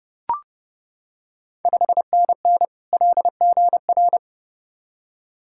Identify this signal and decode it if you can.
Morse code, heard as a keyed tone, about 30 wpm, 690 Hz. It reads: 5NN LGR
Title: Ontario QSO Party Exchange 30wpm